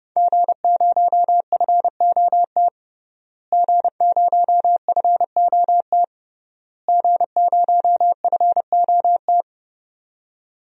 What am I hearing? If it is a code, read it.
Morse code, heard as a keyed tone, about 29 wpm, 700 Hz. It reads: G0FOT G0FOT G0FOT
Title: DXCC-100 Call Signs - 5 Characters - Repeated 3x - 30wpm